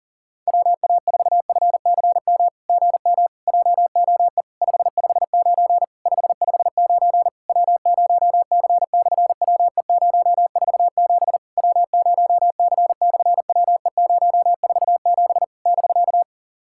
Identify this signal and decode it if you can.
Morse code, heard as a keyed tone, about 40 wpm, 700 Hz. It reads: WA4FCM GM JOE 559 559 W0C/WE047 W0C/WE047 <BK>
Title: SOTA QSO Practice 40wpm